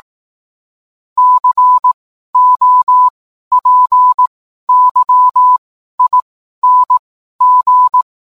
Morse code, heard as a keyed tone, about 18 wpm, spaced as for 12 wpm, 1000 Hz. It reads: COPYING